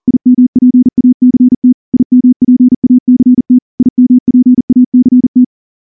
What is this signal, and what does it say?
Morse code, heard as a keyed tone, about 38 wpm, 250 Hz. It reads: IMPACT IMPACT IMPACT
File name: morse / morse-training-40WPM